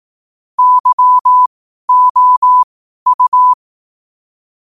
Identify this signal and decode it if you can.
Morse code, heard as a keyed tone, about 18 wpm, spaced as for 12 wpm, 1000 Hz. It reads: YOU
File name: qotdfiles / 12wpm